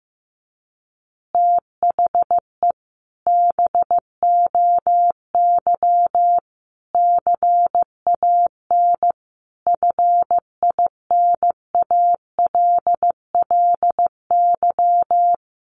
Morse code, heard as a keyed tone, about 15 wpm, 700 Hz. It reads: THE BOY CAN FINALLY
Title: Additional Easy Phrases 15wpm